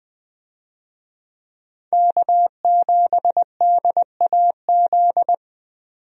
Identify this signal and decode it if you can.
Morse code, heard as a keyed tone, about 21 wpm, 700 Hz. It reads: K7DAZ
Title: USA Call Signs - 5 Characters - 20wpm